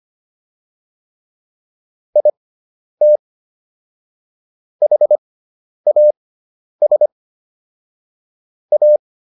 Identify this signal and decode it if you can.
Morse code, heard as a keyed tone, about 25 wpm, spaced as for 10 wpm, 600 Hz. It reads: IT HAS A